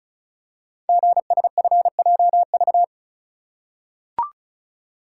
Morse code, heard as a keyed tone, about 36 wpm, 700 Hz. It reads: GSFJV
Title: Code Groups - 5 Letters - 35wpm